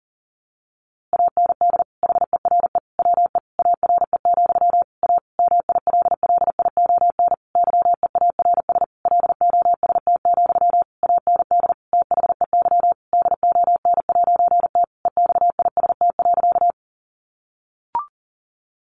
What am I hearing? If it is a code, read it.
Morse code, heard as a keyed tone, about 40 wpm, 700 Hz. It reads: AND HERE WE ARE, A MILLION YEARS LOST, AND THEY DON'T EXIST.
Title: A Sound Of Thunder 40wpm